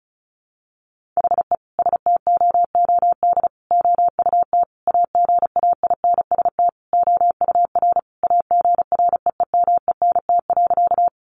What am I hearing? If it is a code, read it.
Morse code, heard as a keyed tone, about 34 wpm, 700 Hz. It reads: HE STOOD OUT AGAINST OUR AGREEMENT.
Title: Sentences from Top 1500 Words 35wpm